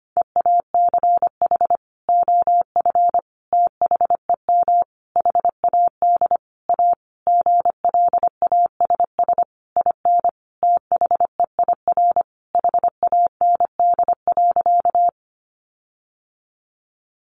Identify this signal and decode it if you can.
Morse code, heard as a keyed tone, about 25 wpm, 700 Hz. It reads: EACH OF THEM HAD A GLASS IN THEIR HAND.
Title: Sentences from Top 1300 Words 25wpm